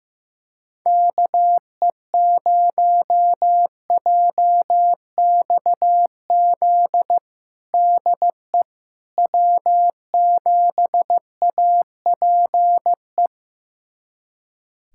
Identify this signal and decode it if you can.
Morse code, heard as a keyed tone, about 15 wpm, 700 Hz. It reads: KE0JXZ DE W7APE
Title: USA Callsign DE Callsign 15wpm